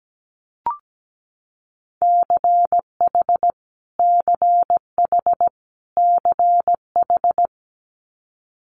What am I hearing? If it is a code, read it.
Morse code, heard as a keyed tone, about 17 wpm, 700 Hz. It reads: CH CH CH
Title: Multi-letter Phonemes Repeated 3x 17wpm